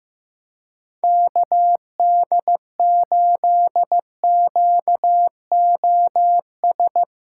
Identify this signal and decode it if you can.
Morse code, heard as a keyed tone, about 15 wpm, 700 Hz. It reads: KD8QOS
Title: USA Call Signs - 6 Characters - 2x Character Spacing - 15wpm